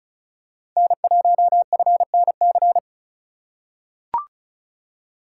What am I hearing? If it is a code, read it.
Morse code, heard as a keyed tone, about 36 wpm, 700 Hz. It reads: N1FNC
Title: USA Call Signs - 5 Characters - 3x Character Spacing - 35wpm